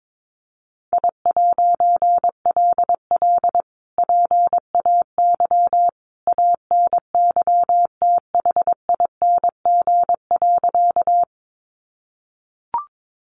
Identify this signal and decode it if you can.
Morse code, heard as a keyed tone, about 22 wpm, 700 Hz. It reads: I'LL PAY ANYTHING.
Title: A Sound Of Thunder 22wpm